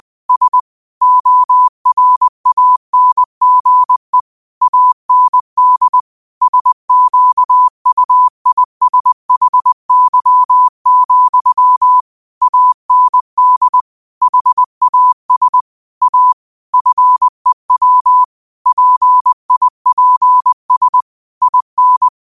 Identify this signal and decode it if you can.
Morse code, heard as a keyed tone, about 20 wpm, 1000 Hz. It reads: S ORANGE AND SQUISHY, AND HAS A FEW PIPS IN
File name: qotdfiles / 20wpm